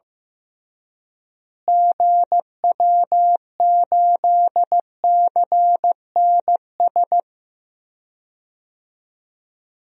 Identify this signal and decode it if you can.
Morse code, heard as a keyed tone, about 15 wpm, 700 Hz. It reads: GW8CNS
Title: DXCC-100 Call Signs - 6 Characters - Repeated 2x - 15wpm